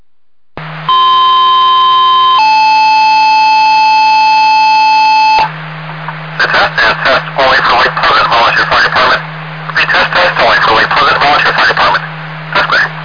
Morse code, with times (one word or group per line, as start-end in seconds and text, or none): none